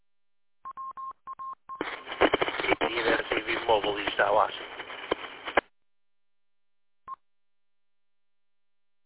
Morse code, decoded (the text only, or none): none